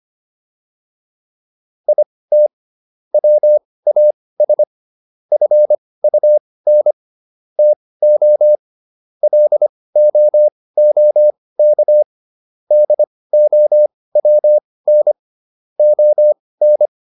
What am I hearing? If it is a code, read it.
Morse code, heard as a keyed tone, about 25 wpm, spaced as for 18 wpm, 600 Hz. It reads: IT WAS FUN TO LOOK DOWN ON